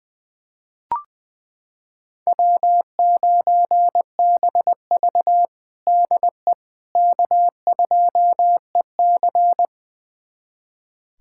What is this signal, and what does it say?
Morse code, heard as a keyed tone, about 20 wpm, 700 Hz. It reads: W9BV DE K2EC
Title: USA Callsign DE Callsign - 4 Characters - 20wpm